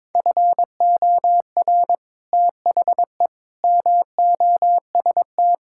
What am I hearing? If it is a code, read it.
Morse code, heard as a keyed tone, about 22 wpm, 700 Hz. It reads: FOR THE MOST
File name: practice / 22wpm / Sentences with 4-Word Phrases - 22wpm